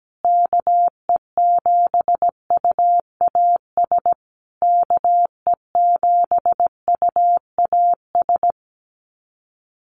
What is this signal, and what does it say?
Morse code, heard as a keyed tone, about 17 wpm, 700 Hz. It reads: KE7UAS KE7UAS
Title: USA Call Signs Repeated 3x 17wpm